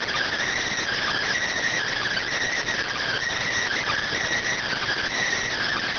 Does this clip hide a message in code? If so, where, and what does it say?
Yes: Emergency Alert System (SAME) tones (header bursts) from 0.8 to 5.1 s, heard as harsh buzzing bursts.